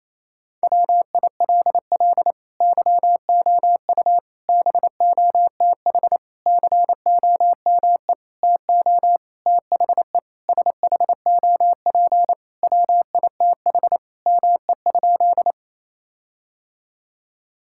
Morse code, heard as a keyed tone, about 28 wpm, 700 Hz. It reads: WILL YOU BOTH COME TO THE SHOP WITH ME?
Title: Sentences from Top 1600 Words 28wpm